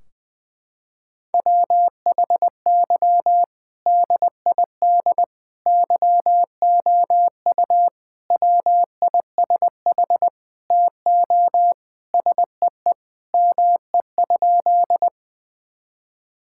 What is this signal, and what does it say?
Morse code, heard as a keyed tone, about 21 wpm, 700 Hz. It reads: WHY DID YOU WISH TO SEE ME?
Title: Sentences from Top 1600 Words 20wpm